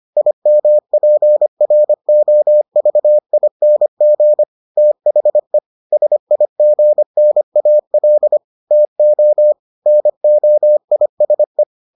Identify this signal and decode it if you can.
Morse code, heard as a keyed tone, about 25 wpm, 600 Hz. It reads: IMPROVING THE SIGNAL TO NOISE